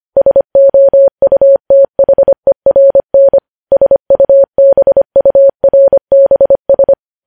Morse code, heard as a keyed tone, about 25 wpm, 550 Hz. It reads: SOUTHERN SUBURBS
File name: podcast / News headlines-1x-25x25